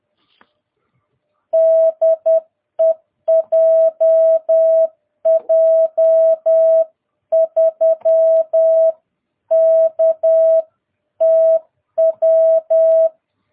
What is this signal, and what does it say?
Morse code, heard as a keyed tone, about 10 wpm, 660 Hz. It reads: DE JJ3KTW
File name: jh8cbh / 2022cw